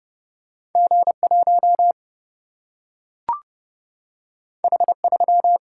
Morse code, heard as a keyed tone, about 30 wpm, 700 Hz. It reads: G 1 H 3
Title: Single-Letter Single-Number 30wpm